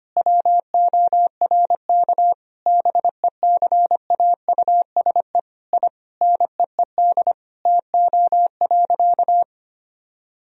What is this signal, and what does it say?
Morse code, heard as a keyed tone, about 25 wpm, 700 Hz. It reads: WORK BECAUSE I NEED TO.
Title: Sentences from Top 1000 Words - Review Part 3 - 25wpm